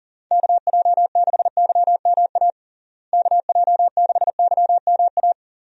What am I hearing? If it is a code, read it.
Morse code, heard as a keyed tone, about 40 wpm, 700 Hz. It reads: KJ6YMA KJ6YMA
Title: USA Call Signs - 6 Characters - Repeated 3x - 40wpm